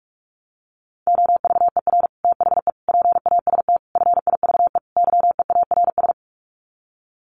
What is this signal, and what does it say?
Morse code, heard as a keyed tone, about 44 wpm, 700 Hz. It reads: OVER THE PAST FIVE YEARS
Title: Sets of 5 Words Encore 45wpm